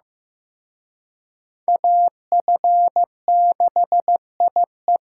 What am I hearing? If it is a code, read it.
Morse code, heard as a keyed tone, about 18 wpm, 700 Hz. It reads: AF6IE
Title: Speed Racing - USA 2x2 Call Signs 15wpm